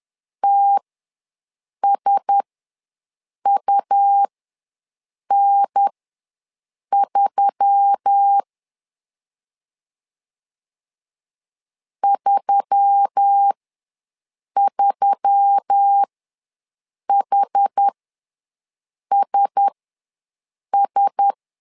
Morse code, heard as a keyed tone, about 10 wpm, spaced as for 6 wpm, 800 Hz. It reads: TSUN3 33HSS
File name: LEZIONI / 4 Q1S3 ws 10wpm 2437